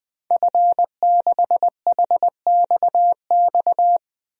Morse code, heard as a keyed tone, about 21 wpm, 700 Hz. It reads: F6HXX
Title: DXCC-100 Call Signs - 5 Characters - 20wpm